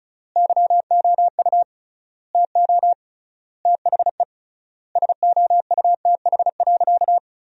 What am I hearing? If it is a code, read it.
Morse code, heard as a keyed tone, about 34 wpm, 700 Hz. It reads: YOU TO THE SOUTH.
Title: 3x Word-Spacing - Sentences from Top 400 Words - 35wpm